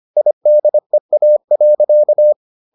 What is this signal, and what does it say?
Morse code, heard as a keyed tone, about 25 wpm, 600 Hz. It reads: IDEA.